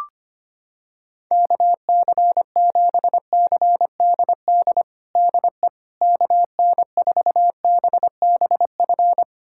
Morse code, heard as a keyed tone, about 25 wpm, 700 Hz. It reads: KC7CDD DE KN4BBF
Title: USA Callsign DE Callsign - 6 Characters - 25wpm